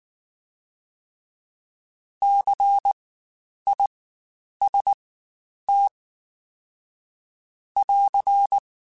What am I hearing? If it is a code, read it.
Morse code, heard as a keyed tone, about 19 wpm, spaced as for 9 wpm, 800 Hz. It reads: CIST <AR>
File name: Spaced Alphas / Spaced Alphas mp3s / Cs sp @18